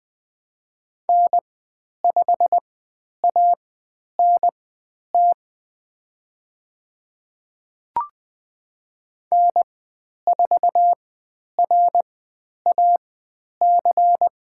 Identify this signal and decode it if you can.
Morse code, heard as a keyed tone, about 20 wpm, spaced as for 10 wpm, 700 Hz. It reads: N5ANT N4RAC
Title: 102 Lesson 17g - Letter u - Call Signs 20wpm